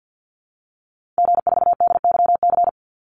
Morse code, heard as a keyed tone, about 50 wpm, 700 Hz. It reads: G4DYC
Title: Call Signs Repeated 2x - 50wpm